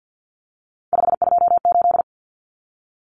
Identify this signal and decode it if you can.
Morse code, heard as a keyed tone, about 50 wpm, 700 Hz. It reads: 528
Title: Common Cut Numbers - 3-Digit 50wpm